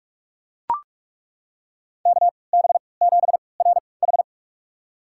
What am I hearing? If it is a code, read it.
Morse code, heard as a keyed tone, about 42 wpm, spaced as for 25 wpm, 700 Hz. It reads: KB7RH
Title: USA Call Signs - 3x Character Spacing - 45wpm